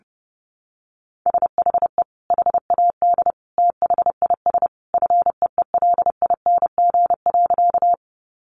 Morse code, heard as a keyed tone, about 30 wpm, 700 Hz. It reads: SHE HAD THIS FEELING.
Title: Sentences from Top 1200 Words 30wpm